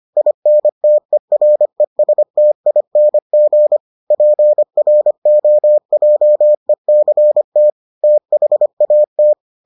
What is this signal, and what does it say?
Morse code, heard as a keyed tone, about 25 wpm, 600 Hz. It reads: INTERESTING PROJECT THAT